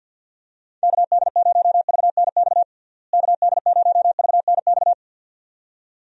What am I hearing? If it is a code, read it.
Morse code, heard as a keyed tone, about 46 wpm, 700 Hz. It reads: KD0VNX KD0VNX
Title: USA Call Signs - 6 Characters - Repeated 2x - 50wpm